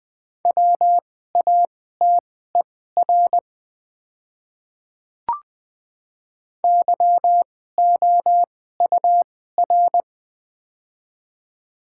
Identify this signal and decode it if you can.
Morse code, heard as a keyed tone, about 20 wpm, spaced as for 14 wpm, 700 Hz. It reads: WATER YOUR
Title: 2x Character Spacing - Top 300 Words - 20wpm